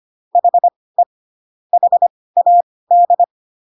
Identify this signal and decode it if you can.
Morse code, heard as a keyed tone, about 25 wpm, spaced as for 18 wpm, 700 Hz. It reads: HE HAD